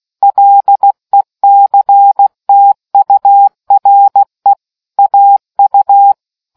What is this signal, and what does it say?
Morse code, heard as a keyed tone, about 16 wpm, 800 Hz. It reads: LECTURE AU